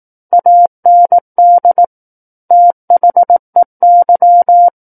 Morse code, heard as a keyed tone, about 18 wpm, 700 Hz. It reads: AND THEY
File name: Book One / 09-Chapter 9 - 18 WPM